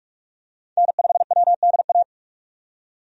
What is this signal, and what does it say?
Morse code, heard as a keyed tone, about 46 wpm, 700 Hz. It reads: N5WDA
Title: USA Call Signs - 5 Characters - 45wpm